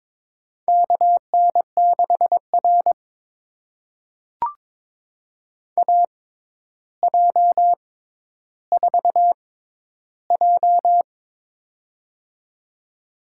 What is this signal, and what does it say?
Morse code, heard as a keyed tone, about 22 wpm, spaced as for 11 wpm, 700 Hz. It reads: KN6R AJ4J
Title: USA Call Signs - 4 Characters - 6x Character Spacing - 22wpm